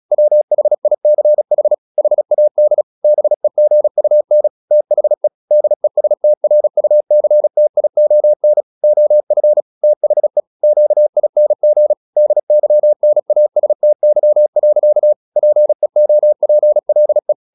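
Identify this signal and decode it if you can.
Morse code, heard as a keyed tone, about 36 wpm, 600 Hz. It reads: WHICH HAD BEGUN THE DESTRUCTION OF THE QING DYNASTY. PEOPLE
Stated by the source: Morse practice at 36 wpm, text of Ken Follett's novel Never, from CW med Gnister.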